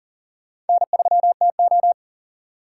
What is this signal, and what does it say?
Morse code, heard as a keyed tone, about 52 wpm, 700 Hz. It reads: N3TO